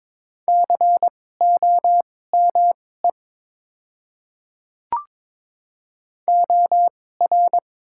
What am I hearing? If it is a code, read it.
Morse code, heard as a keyed tone, about 22 wpm, spaced as for 16 wpm, 700 Hz. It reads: COME OR
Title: 2x Character Spacing - Top 100 Words - 22wpm